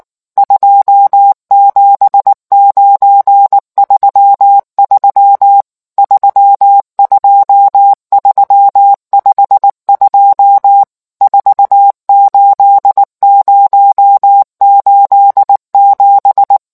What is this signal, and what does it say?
Morse code, heard as a keyed tone, about 19 wpm, 800 Hz. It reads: 27933 32352 48087